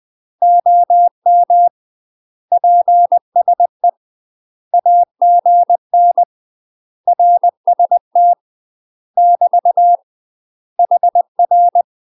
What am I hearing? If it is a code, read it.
Morse code, heard as a keyed tone, about 20 wpm, 700 Hz. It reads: OM PSE AGN RST <BT> HR
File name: UK MASTER QSOs TRAINING / 20WPM / UK MASTER QSOs 0074